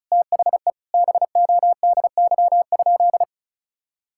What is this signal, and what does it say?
Morse code, heard as a keyed tone, about 36 wpm, 700 Hz. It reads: THE BODY?
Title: Sentences from Top 1000 Words - Review Part 6 - 35wpm